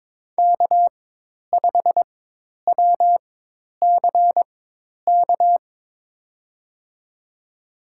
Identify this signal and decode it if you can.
Morse code, heard as a keyed tone, about 21 wpm, spaced as for 10 wpm, 700 Hz. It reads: K5WCK
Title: USA Call Signs - 5 Characters - 4x Character Spacing - 22wpm